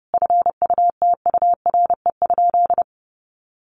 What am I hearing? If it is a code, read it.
Morse code, heard as a keyed tone, about 30 wpm, 700 Hz. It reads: FUTURE?
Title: Sentences from Top 700 Words 30wpm